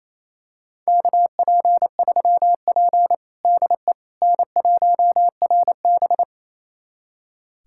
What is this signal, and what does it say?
Morse code, heard as a keyed tone, about 28 wpm, 700 Hz. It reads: KP3P DE N1RB
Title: USA Callsign DE Callsign - 4 Characters - 28wpm